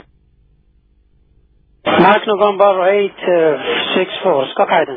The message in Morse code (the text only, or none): none